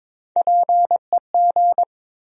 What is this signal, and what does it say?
Morse code, heard as a keyed tone, about 22 wpm, 700 Hz. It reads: PEG
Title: USA Common Names - Max 3 Letters - 22wpm